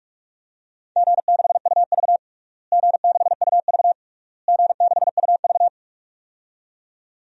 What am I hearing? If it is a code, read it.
Morse code, heard as a keyed tone, about 42 wpm, 700 Hz. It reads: G6UV G6UV G6UV
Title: DXCC-100 Call Signs - 4 Characters - Repeated 3x - 45wpm